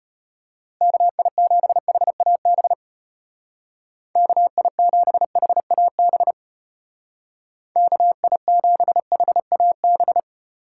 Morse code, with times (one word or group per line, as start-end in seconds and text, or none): 0.8-2.7 KI7HAB
4.2-6.3 KI7HAB
7.8-10.2 KI7HAB